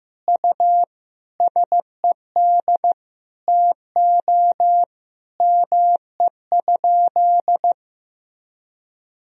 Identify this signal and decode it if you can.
Morse code, heard as a keyed tone, about 15 wpm, 700 Hz. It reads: U SED TO ME?
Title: Sentences from Top 200 Words and Abbreviations 15wpm